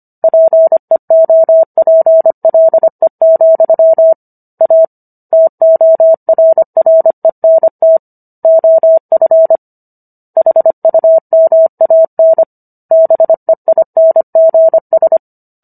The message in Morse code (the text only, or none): PEOPLE, A TORRENT OF HUMAN BEINGS